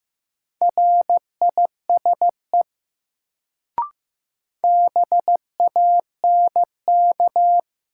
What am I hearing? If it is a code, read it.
Morse code, heard as a keyed tone, about 15 wpm, 700 Hz. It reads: RISE BANK
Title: Sentences from Top 900 Words - New Word Drill - 15wpm